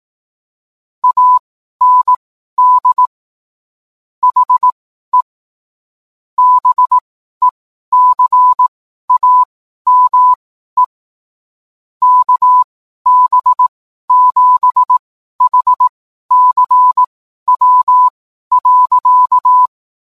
Morse code, heard as a keyed tone, about 18 wpm, spaced as for 12 wpm, 1000 Hz. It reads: AND HE BECAME KB7HCW.